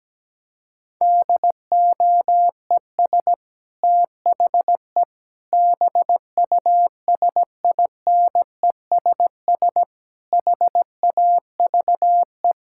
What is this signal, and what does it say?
Morse code, heard as a keyed tone, about 17 wpm, 700 Hz. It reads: DOES THE BUSINESS HAVE
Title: Sentences from Top 1600 Words 17wpm